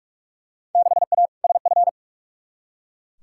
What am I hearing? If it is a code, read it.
Morse code, heard as a keyed tone, about 43 wpm, 700 Hz. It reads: 6A SF
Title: Field Day Exchanges - 45wpm